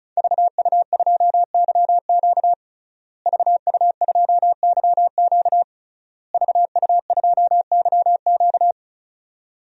Morse code, heard as a keyed tone, about 36 wpm, 700 Hz. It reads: VU2YQ VU2YQ VU2YQ
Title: DXCC-100 Call Signs - 5 Characters - Repeated 3x - 35wpm